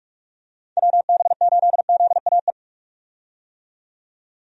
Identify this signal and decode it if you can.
Morse code, heard as a keyed tone, about 46 wpm, 700 Hz. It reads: WB8ZAE
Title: USA Call Signs - 6 Characters - 45wpm